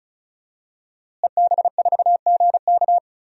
Morse code, heard as a keyed tone, about 36 wpm, 700 Hz. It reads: EB4GK
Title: DXCC-100 Call Signs - 5 Characters - Repeated 2x - 35wpm